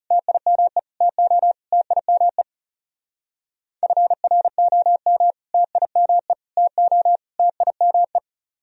Morse code, heard as a keyed tone, about 37 wpm, 700 Hz. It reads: TIME TO TIME FROM TIME TO TIME
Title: Speed-Racing - Sets of 4 Words 30wpm